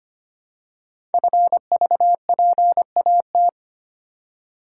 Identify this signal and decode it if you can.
Morse code, heard as a keyed tone, about 25 wpm, 700 Hz. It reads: FVPAT
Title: Code Groups - 5 Letters - 25wpm